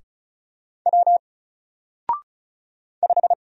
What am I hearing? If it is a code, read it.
Morse code, heard as a keyed tone, about 36 wpm, 700 Hz. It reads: W 5
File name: Single-Letter-Number-35wpm